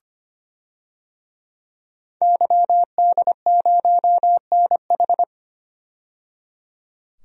Morse code, heard as a keyed tone, about 25 wpm, 700 Hz. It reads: YD0NH
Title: DXCC-100 Call Signs - 5 Characters - 25wpm